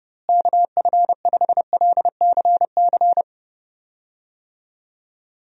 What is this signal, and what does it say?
Morse code, heard as a keyed tone, about 30 wpm, 700 Hz. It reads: KF5LCC
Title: USA Call Signs - 6 Characters - Repeated 2x - 30wpm